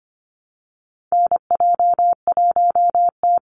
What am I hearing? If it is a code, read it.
Morse code, heard as a keyed tone, about 25 wpm, 700 Hz. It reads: NJ1T
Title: Call Signs Repeated 3x - 25wpm